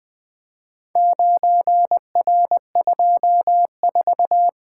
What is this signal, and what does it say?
Morse code, heard as a keyed tone, about 20 wpm, 700 Hz. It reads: 9R24
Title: Four Letter-Number Combinations - 20wpm